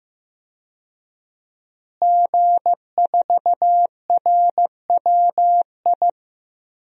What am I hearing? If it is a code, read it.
Morse code, heard as a keyed tone, about 15 wpm, 700 Hz. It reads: G4RWI
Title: DXCC-100 Call Signs - 5 Characters - Repeated 2x - 15wpm